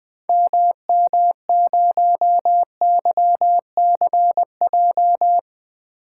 Morse code, heard as a keyed tone, about 20 wpm, 700 Hz. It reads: MM0YCJ
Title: 2023 SOTA Activator Call Signs - 20wpm